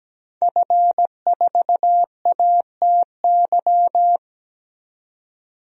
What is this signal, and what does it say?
Morse code, heard as a keyed tone, about 17 wpm, 700 Hz. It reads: F4ATY
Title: DXCC-100 Call Signs - 5 Characters - 17wpm